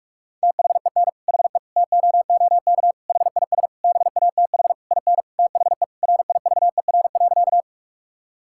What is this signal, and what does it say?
Morse code, heard as a keyed tone, about 46 wpm, 700 Hz. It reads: THEN HE TOOK HIS BATH IN THE RIVER.
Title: Sentences from Top 1600 Words 45wpm